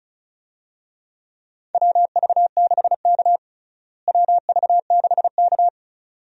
Morse code, heard as a keyed tone, about 36 wpm, 700 Hz. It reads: WV6K WV6K
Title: USA Call Signs - 4 Characters - Repeated 2x - 35wpm